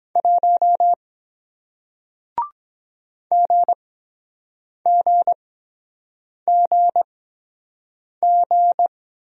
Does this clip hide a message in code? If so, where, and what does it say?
Yes: Morse code from 0.2 to 8.9 s, heard as a keyed tone, about 23 wpm, spaced as for 7 wpm, 700 Hz. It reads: 1 GGGG